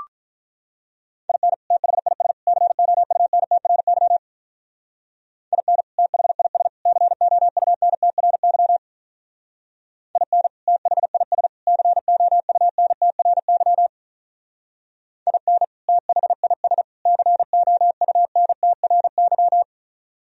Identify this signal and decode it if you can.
Morse code, heard as a keyed tone, about 42 wpm, 700 Hz. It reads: IN THIS COUNTRY IN THIS COUNTRY IN THIS COUNTRY IN THIS COUNTRY
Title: Speed-Racing - Sets of 3 Words 35wpm